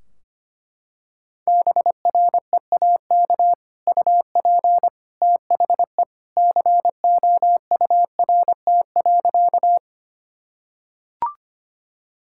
Morse code, heard as a keyed tone, about 25 wpm, 700 Hz. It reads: BREAK UP THE COURT.
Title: Sentences from Top 1600 Words 25wpm